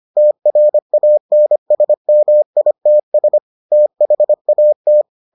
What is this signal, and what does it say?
Morse code, heard as a keyed tone, about 25 wpm, 600 Hz. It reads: TRANSMITS THAT